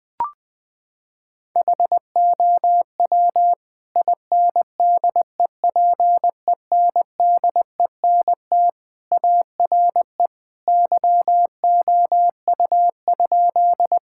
Morse code, heard as a keyed tone, about 20 wpm, 700 Hz. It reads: HOW INDEPENDENT ARE YOU?
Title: Sentences from Top 1000 Words - Review Part 6 - 20wpm